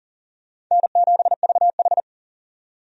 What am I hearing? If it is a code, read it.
Morse code, heard as a keyed tone, about 40 wpm, 700 Hz. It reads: N7VH
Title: USA Call Signs - 4 Characters - 40wpm